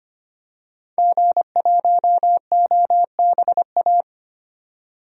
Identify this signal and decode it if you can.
Morse code, heard as a keyed tone, about 25 wpm, 700 Hz. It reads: G1OBA
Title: DXCC-100 Call Signs - 5 Characters - 25wpm